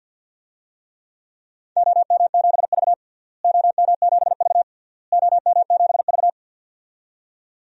Morse code, heard as a keyed tone, about 49 wpm, 700 Hz. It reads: OM7V OM7V OM7V
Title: DXCC-100 Call Signs - 4 Characters - Repeated 3x - 50wpm